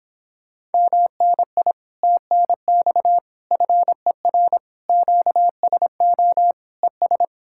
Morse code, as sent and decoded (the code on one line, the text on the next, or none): -- -. .. / - -. -..- / ..-. .-. / --.- ... --- / . ...
MNI TNX FER QSO ES